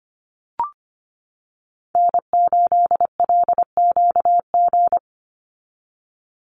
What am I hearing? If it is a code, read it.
Morse code, heard as a keyed tone, about 25 wpm, 700 Hz. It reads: N8LQG
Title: USA Call Signs 25wpm